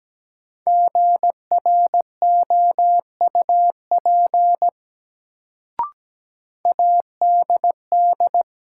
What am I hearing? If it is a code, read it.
Morse code, heard as a keyed tone, about 17 wpm, 700 Hz. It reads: GROUP ADD
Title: Sentences from Top 1000 Words - Review Part 1 - New Word Drill - 17wpm